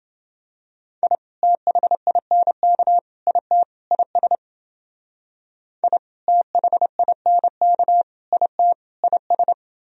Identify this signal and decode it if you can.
Morse code, heard as a keyed tone, about 28 wpm, 700 Hz. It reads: I THINK IT IS I THINK IT IS